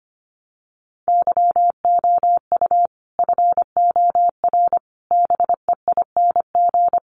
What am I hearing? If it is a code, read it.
Morse code, heard as a keyed tone, about 25 wpm, 700 Hz. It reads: YOU FOR BEING
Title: Sets of 5 Words 25wpm